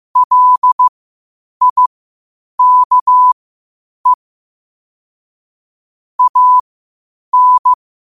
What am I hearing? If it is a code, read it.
Morse code, heard as a keyed tone, about 15 wpm, spaced as for 8 wpm, 1000 Hz. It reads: LIKE AN
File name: qotdfiles / 8wpm